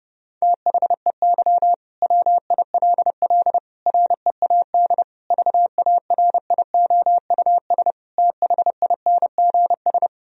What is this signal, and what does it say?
Morse code, heard as a keyed tone, about 32 wpm, 700 Hz. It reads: THEY WILL READ VARIOUS THINGS
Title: Sentences from Top 1000 Words - Review Part 4 - 30wpm